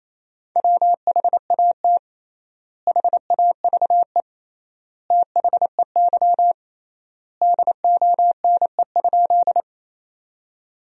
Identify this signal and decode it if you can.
Morse code, heard as a keyed tone, about 28 wpm, 700 Hz. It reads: WHAT HAVE THEY DONE?
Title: 3x Word-Spacing - Sentences from Top 500 Words - 28wpm